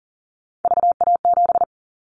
Morse code, heard as a keyed tone, about 38 wpm, 700 Hz. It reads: VA7
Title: Callsign Prefixes for Top 50 DXCC - 40wpm